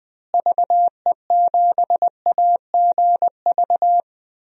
Morse code, heard as a keyed tone, about 20 wpm, 700 Hz. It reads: VE7AGV
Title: DXCC-100 Call Signs - 6 Characters - Repeated 3x - 20wpm